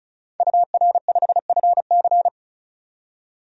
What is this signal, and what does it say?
Morse code, heard as a keyed tone, about 36 wpm, 700 Hz. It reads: UR5FC
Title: DXCC-100 Call Signs - 5 Characters - 35wpm